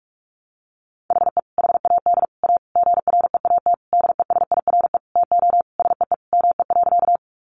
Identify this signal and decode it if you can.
Morse code, heard as a keyed tone, about 44 wpm, 700 Hz. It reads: HE HAD A GREAT DESIRE TO SEE ME.
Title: Sentences from Top 1100 Words 45wpm